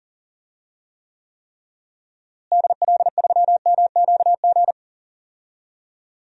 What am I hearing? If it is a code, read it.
Morse code, heard as a keyed tone, about 37 wpm, 700 Hz. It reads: DL3MQG
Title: DXCC-100 Call Signs - 6 Characters - Repeated 3x - 40wpm